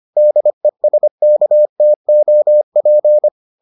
Morse code, heard as a keyed tone, about 25 wpm, 600 Hz. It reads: DESKTOP